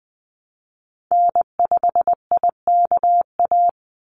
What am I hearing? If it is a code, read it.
Morse code, heard as a keyed tone, about 20 wpm, 700 Hz. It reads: N5IKA
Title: USA Call Signs 20wpm